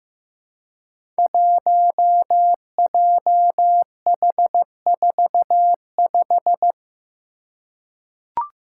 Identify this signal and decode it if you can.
Morse code, heard as a keyed tone, about 15 wpm, 700 Hz. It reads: 1JH45
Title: Five Challenging Letter-Number Combinations 15wpm